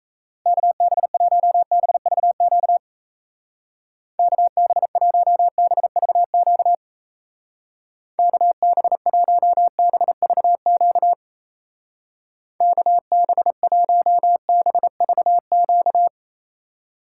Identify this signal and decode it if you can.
Morse code, heard as a keyed tone, about 37 wpm, 700 Hz. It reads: KB1BVQ KB1BVQ KB1BVQ KB1BVQ